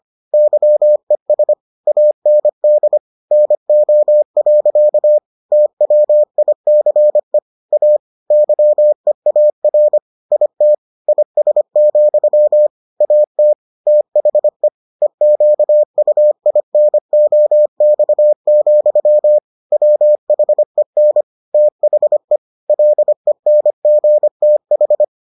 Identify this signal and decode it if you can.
Morse code, heard as a keyed tone, about 25 wpm, 600 Hz. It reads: YES AND NO. TWICE A YEAR IT IS, AT THE EQUINOX, WHEN THE LENGTH